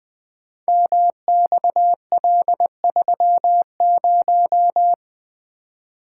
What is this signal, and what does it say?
Morse code, heard as a keyed tone, about 20 wpm, 700 Hz. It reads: MXL30
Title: Code Groups - 5 Letter:Numbers - 20wpm